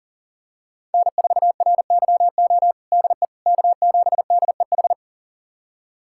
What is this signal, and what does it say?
Morse code, heard as a keyed tone, about 40 wpm, 700 Hz. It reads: N4RYO DE K7DEH
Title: USA Callsign DE Callsign - 5 Characters - 40wpm